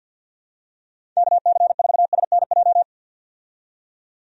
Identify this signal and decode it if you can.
Morse code, heard as a keyed tone, about 47 wpm, 700 Hz. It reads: KC4SNJ
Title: USA Call Signs - 6 Characters - 50wpm